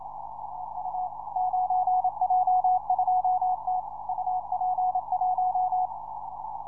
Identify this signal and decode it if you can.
Morse code, heard as a keyed tone, about 26 wpm, 740 Hz. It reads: EU 9J2T UP1